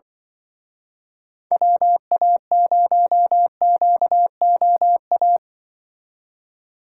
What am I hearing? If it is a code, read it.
Morse code, heard as a keyed tone, about 25 wpm, 700 Hz. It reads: WA0QOA
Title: Speed Racing - USA 2x3 Call Signs 20wpm